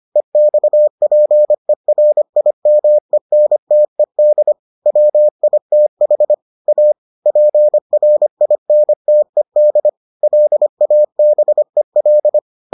Morse code, heard as a keyed tone, about 25 wpm, 600 Hz. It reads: EXPERIMENTED WITH A PRINTED LABEL